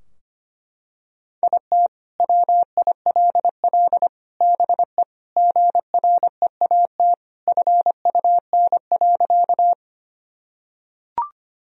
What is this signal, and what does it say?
Morse code, heard as a keyed tone, about 25 wpm, 700 Hz. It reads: IT WILL BE GREAT FUN.
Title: Sentences from Top 1000 Words - Review Part 4 - 25wpm